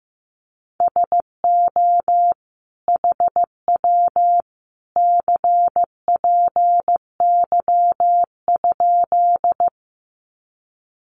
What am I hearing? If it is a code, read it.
Morse code, heard as a keyed tone, about 15 wpm, 700 Hz. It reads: SO HW CPY?
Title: Common QSO Phrases 15wpm